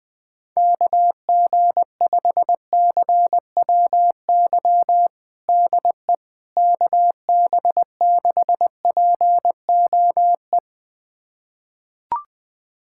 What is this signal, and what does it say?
Morse code, heard as a keyed tone, about 21 wpm, 700 Hz. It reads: KG5CWY DE KB6POE
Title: USA Callsign DE Callsign - 6 Characters - 20wpm